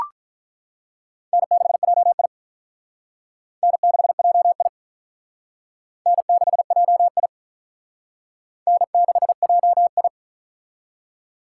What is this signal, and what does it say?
Morse code, heard as a keyed tone, about 48 wpm, 700 Hz. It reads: N6JI N6JI N6JI N6JI